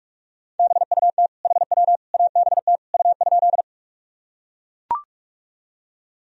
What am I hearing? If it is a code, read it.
Morse code, heard as a keyed tone, about 43 wpm, 700 Hz. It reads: BUT HW ABT U?
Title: Sentences from Top 100 Words and Abbreviations 45wpm